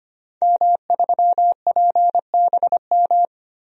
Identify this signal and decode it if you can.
Morse code, heard as a keyed tone, about 25 wpm, 700 Hz. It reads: M3PBM
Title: DXCC-100 Call Signs - 5 Characters - Repeated 2x - 25wpm